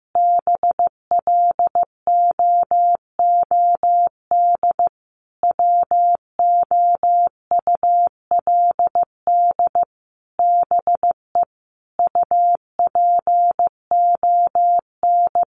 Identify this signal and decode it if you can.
Morse code, heard as a keyed tone, about 15 wpm, 700 Hz. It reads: BLOOD WOULD BE UPON
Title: Sentences from Top 1100 Words 15wpm